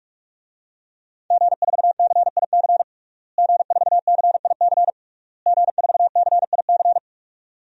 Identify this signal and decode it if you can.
Morse code, heard as a keyed tone, about 41 wpm, 700 Hz. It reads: G4CIC G4CIC G4CIC
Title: DXCC-100 Call Signs - 5 Characters - Repeated 3x - 45wpm